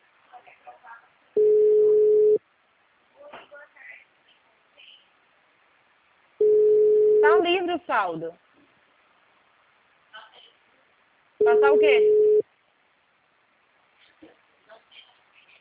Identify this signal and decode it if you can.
Ringback tone (Europe, 425 Hz), with speech.